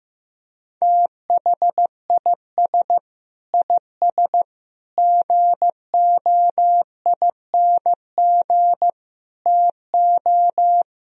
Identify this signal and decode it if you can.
Morse code, heard as a keyed tone, about 15 wpm, 700 Hz. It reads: THIS IS GOING TO
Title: Speed-Racing - Sets of 5 Words 15wpm